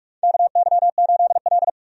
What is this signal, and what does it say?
Morse code, heard as a keyed tone, about 47 wpm, 700 Hz. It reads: KY8L
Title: USA Call Signs - 4 Characters - 45wpm